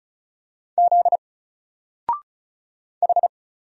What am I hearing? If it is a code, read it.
Morse code, heard as a keyed tone, about 34 wpm, 700 Hz. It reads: Z H
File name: Single-Letter-Number-35wpm